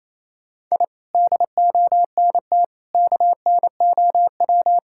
Morse code, heard as a keyed tone, about 28 wpm, 700 Hz. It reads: I DONT KNOW
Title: Sentences from Top 1000 Words - Review Part 5 - 28wpm